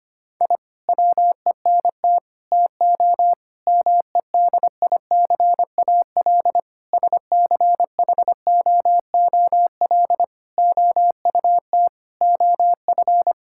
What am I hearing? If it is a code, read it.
Morse code, heard as a keyed tone, about 25 wpm, 700 Hz. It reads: I WENT TO MEDICAL SCHOOL OUT OF
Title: Sentences from Top 1000 Words - Review Part 5 - 25wpm